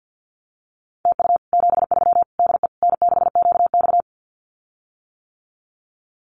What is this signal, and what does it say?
Morse code, heard as a keyed tone, about 50 wpm, 700 Hz. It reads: TU 73 DE N6QX
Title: Common QSO Phrases Encore 50wpm